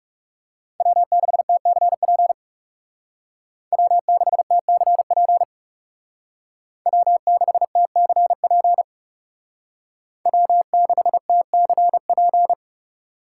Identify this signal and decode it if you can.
Morse code, heard as a keyed tone, about 36 wpm, 700 Hz. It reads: W6TCP W6TCP W6TCP W6TCP
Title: Speed Racing - USA 1x3 Call Signs 30wpm